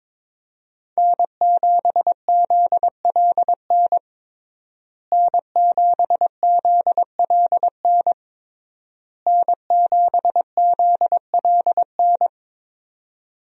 Morse code, heard as a keyed tone, about 22 wpm, 700 Hz. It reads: N7ZLN N7ZLN N7ZLN
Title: USA Call Signs - 5 Characters - Repeated 3x - 22wpm